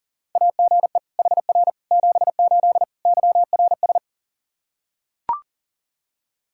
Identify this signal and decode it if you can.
Morse code, heard as a keyed tone, about 40 wpm, 700 Hz. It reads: AGE HR 78 YRS
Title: Practice QSO - Exchange 3 - More Info - 40wpm